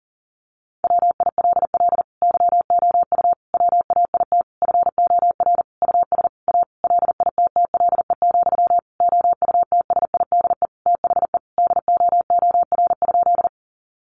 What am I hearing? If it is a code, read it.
Morse code, heard as a keyed tone, about 40 wpm, 700 Hz. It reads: WILL YOU WAIT FOR US A LITTLE, OUTSIDE THE DOOR?
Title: Sentences from Top 900 Words 40wpm